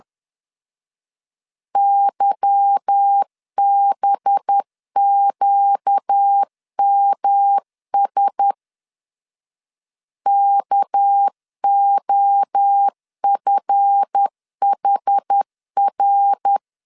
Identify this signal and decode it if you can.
Morse code, heard as a keyed tone, about 10 wpm, 800 Hz. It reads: YBQMS KOFHR
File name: LEZIONI / 8 YJ45 / 8 YJ4 10wpm 2633